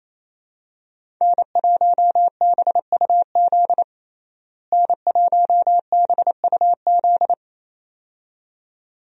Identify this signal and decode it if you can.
Morse code, heard as a keyed tone, about 28 wpm, 700 Hz. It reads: N1BUZ N1BUZ
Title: USA Call Signs - 5 Characters - Repeated 2x - 28wpm